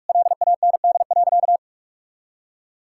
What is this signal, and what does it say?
Morse code, heard as a keyed tone, about 45 wpm, 700 Hz. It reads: LAND.
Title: Sentences from Top 1000 Words - Review Part 6 - 45wpm